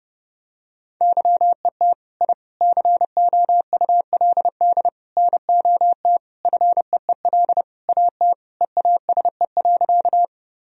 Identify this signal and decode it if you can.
Morse code, heard as a keyed tone, about 30 wpm, 700 Hz. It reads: YET I COULD NOT FEEL AT EASE.
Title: Sentences from Top 1000 Words - Review Part 1 - 30wpm